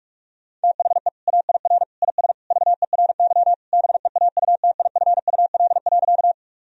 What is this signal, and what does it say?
Morse code, heard as a keyed tone, about 46 wpm, 700 Hz. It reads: THE AIR IS VERY BEAUTIFUL.
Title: Sentences from Top 1000 Words - Review Part 1 - 45wpm